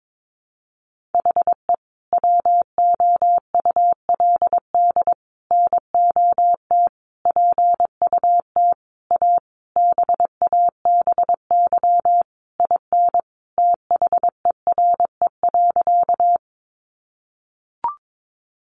Morse code, heard as a keyed tone, about 22 wpm, 700 Hz. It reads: HE WOULD NOT PUT A BABY IN THERE.
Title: Easy Phrases 22wpm